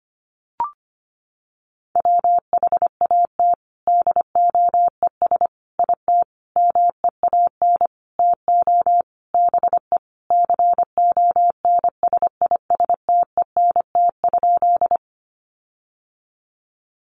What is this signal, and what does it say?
Morse code, heard as a keyed tone, about 25 wpm, 700 Hz. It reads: WHAT DOES IT MEAN TO BE CONSISTENT?
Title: Sentences from Top 1500 Words 25wpm